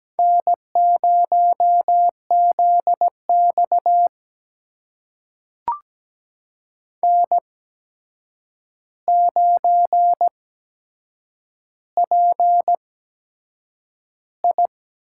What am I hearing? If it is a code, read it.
Morse code, heard as a keyed tone, about 17 wpm, spaced as for 7 wpm, 700 Hz. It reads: N0ZX N9PI
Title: USA Call Signs - 4 Characters - 8x Character Spacing - 17wpm